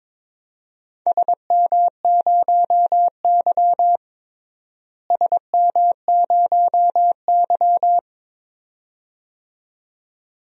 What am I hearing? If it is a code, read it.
Morse code, heard as a keyed tone, about 22 wpm, 700 Hz. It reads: SM0Y SM0Y
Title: DXCC-100 Call Signs - 4 Characters - Repeated 2x - 22wpm